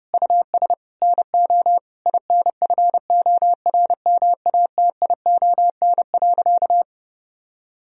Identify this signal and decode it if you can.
Morse code, heard as a keyed tone, about 30 wpm, 700 Hz. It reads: US NO INFORMATION.
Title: Sentences from Top 1000 Words - Review Part 3 - 30wpm